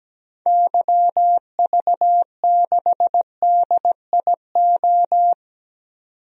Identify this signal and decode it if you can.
Morse code, heard as a keyed tone, about 17 wpm, 700 Hz. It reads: YV6DIO
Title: DXCC-100 Call Signs - 6 Characters - 17wpm